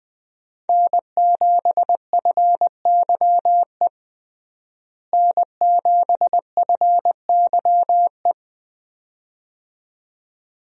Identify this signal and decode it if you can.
Morse code, heard as a keyed tone, about 21 wpm, 700 Hz. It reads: N7FYE N7FYE
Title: USA Call Signs - 5 Characters - Repeated 2x - 20wpm